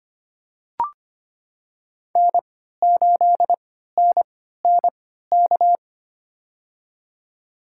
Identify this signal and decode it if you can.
Morse code, heard as a keyed tone, about 25 wpm, spaced as for 14 wpm, 700 Hz. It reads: N8NNK
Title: USA Call Signs - 3x Character Spacing - 25wpm